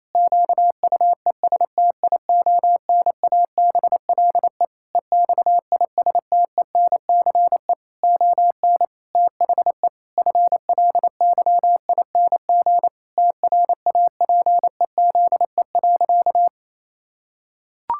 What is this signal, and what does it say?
Morse code, heard as a keyed tone, about 28 wpm, 700 Hz. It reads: QUESTIONABLE EXISTENCE ON THE FLYING TRAPEZE.
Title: Pangram Challenge v2 - 28wpm